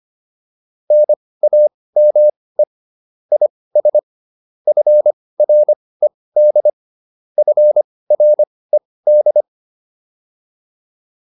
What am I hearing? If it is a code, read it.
Morse code, heard as a keyed tone, about 25 wpm, spaced as for 18 wpm, 600 Hz. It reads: NAME IS FRED FRED